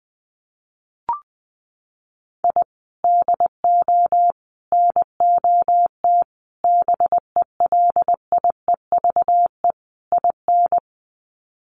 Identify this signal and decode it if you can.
Morse code, heard as a keyed tone, about 19 wpm, 700 Hz. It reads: I DO NOT BELIEVE IN
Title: Sets of 5 Words Encore 20wpm